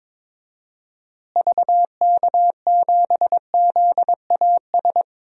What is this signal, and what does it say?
Morse code, heard as a keyed tone, about 22 wpm, 700 Hz. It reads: VK7ZAS
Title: DXCC-100 Call Signs - 6 Characters - Repeated 2x - 22wpm